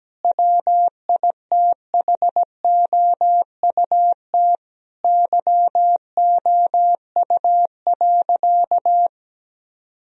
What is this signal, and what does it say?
Morse code, heard as a keyed tone, about 17 wpm, 700 Hz. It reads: WITHOUT YOU.
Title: Sentences from Top 1000 Words - Review Part 3 - 17wpm